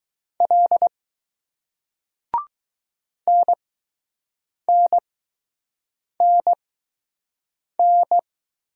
Morse code, heard as a keyed tone, about 20 wpm, spaced as for 6 wpm, 700 Hz. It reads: L NNNN